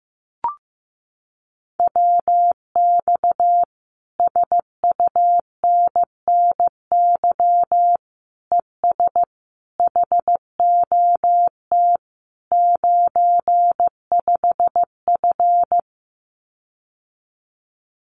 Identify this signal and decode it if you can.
Morse code, heard as a keyed tone, about 15 wpm, 700 Hz. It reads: WX SUNNY ES HOT 95F
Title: Common QSO Phrases Encore 15wpm